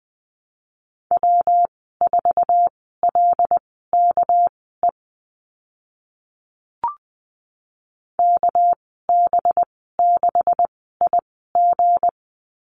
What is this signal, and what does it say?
Morse code, heard as a keyed tone, about 20 wpm, spaced as for 14 wpm, 700 Hz. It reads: W4LKE KB6IG
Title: 200 Lesson 32g - Letter k - Call Signs 20wpm